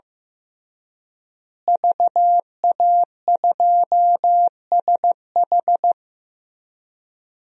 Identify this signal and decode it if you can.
Morse code, heard as a keyed tone, about 15 wpm, 700 Hz. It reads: VA2SH
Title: DXCC-100 Call Signs - 5 Characters - Repeated 3x - 15wpm